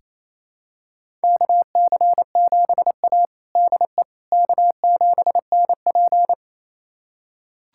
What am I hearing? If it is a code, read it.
Morse code, heard as a keyed tone, about 28 wpm, 700 Hz. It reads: KC7A DE K7NP
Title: USA Callsign DE Callsign - 4 Characters - 28wpm